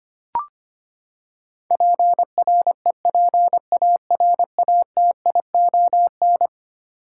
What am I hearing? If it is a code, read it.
Morse code, heard as a keyed tone, about 25 wpm, 700 Hz. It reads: PREPARATION
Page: morse-code-ninja-practice-25wpm